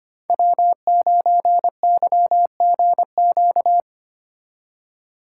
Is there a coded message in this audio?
Yes: Morse code from 0.3 to 3.8 s, heard as a keyed tone, about 32 wpm, 700 Hz. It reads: W9YGQ